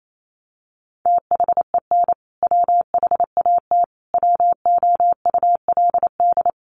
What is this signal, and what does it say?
Morse code, heard as a keyed tone, about 28 wpm, 700 Hz. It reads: THEN WHAT WOULD
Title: Sentences from Top 400 Words 28wpm